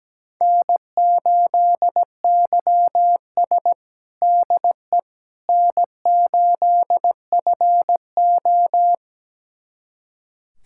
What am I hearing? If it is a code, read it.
Morse code, heard as a keyed tone, about 17 wpm, 700 Hz. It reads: N8YS DE N8FO
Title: USA Callsign DE Callsign - 4 Characters - 17wpm